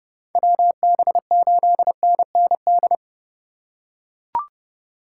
Morse code, heard as a keyed tone, about 28 wpm, 700 Hz. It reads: WB8NND